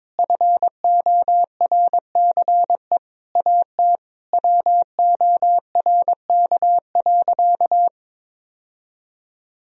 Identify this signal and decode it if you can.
Morse code, heard as a keyed tone, about 22 wpm, 700 Hz. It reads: FORCE AT WORK.
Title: Sentences from Top 1000 Words - Review Part 4 - 22wpm